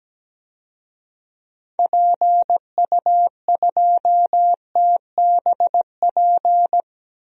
Morse code, heard as a keyed tone, about 17 wpm, 700 Hz. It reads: PU2TBP
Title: DXCC-100 Call Signs - 6 Characters - Repeated 2x - 17wpm